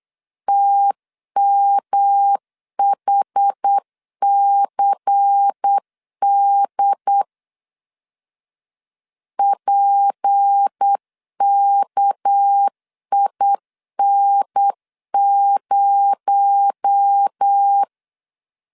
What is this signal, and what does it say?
Morse code, heard as a keyed tone, about 8 wpm, 800 Hz. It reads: TMHCD PKIN0
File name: LEZIONI / 6 PKX0 / 6 PKX0 8wpm 3341